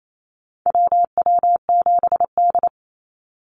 Morse code, heard as a keyed tone, about 28 wpm, 700 Hz. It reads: WW7D
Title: NA SOTA Activator Callsigns 28wpm